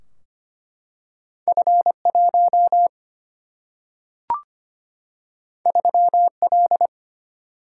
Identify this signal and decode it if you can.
Morse code, heard as a keyed tone, about 25 wpm, 700 Hz. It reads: F1 3L
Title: Single-Letter Single-Number 25wpm